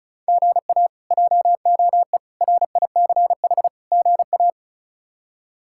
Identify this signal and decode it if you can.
Morse code, heard as a keyed tone, about 36 wpm, 700 Hz. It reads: GA JOE RICH GA
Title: K1USN SST QSO Practice - 35wpm